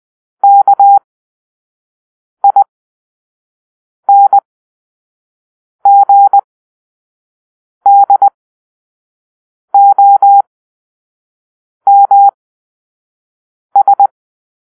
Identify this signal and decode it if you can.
Morse code, heard as a keyed tone, about 20 wpm, spaced as for 5 wpm, 800 Hz. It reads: KINGDOMS
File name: Grp-5 Drill-2